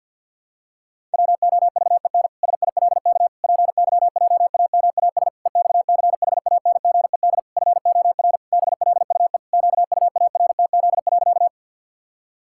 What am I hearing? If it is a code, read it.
Morse code, heard as a keyed tone, about 51 wpm, 700 Hz. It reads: WOVEN SILK PYJAMAS EXCHANGED FOR BLUE QUARTZ.
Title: Pangram Challenge v2 - 50wpm